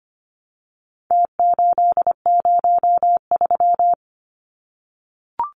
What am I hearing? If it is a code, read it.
Morse code, heard as a keyed tone, about 25 wpm, 700 Hz. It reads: T803
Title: Numbers - Contest Style - 4-Digit 25wpm